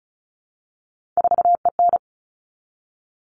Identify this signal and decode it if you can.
Morse code, heard as a keyed tone, about 34 wpm, 700 Hz. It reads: 4EN
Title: All Cut Numbers - 3-Digit 35wpm